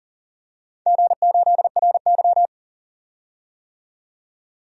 Morse code, heard as a keyed tone, about 41 wpm, 700 Hz. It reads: G8RY
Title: DXCC-100 Call Signs - 4 Characters - 40wpm